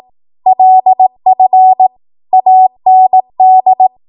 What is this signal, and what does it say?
Morse code, heard as a keyed tone, about 18 wpm, 750 Hz. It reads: LF AND
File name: Morse / 18WPM